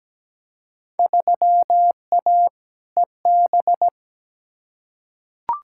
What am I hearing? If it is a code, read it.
Morse code, heard as a keyed tone, about 17 wpm, 700 Hz. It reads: 3A EB
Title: Field Day Exchanges Encore - 17wpm